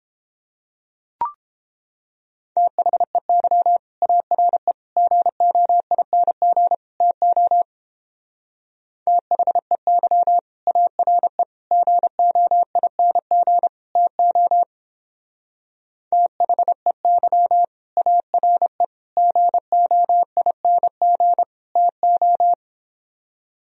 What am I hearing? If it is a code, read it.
Morse code, heard as a keyed tone, about 31 wpm, 700 Hz. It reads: THEY ARE GOING TO THEY ARE GOING TO THEY ARE GOING TO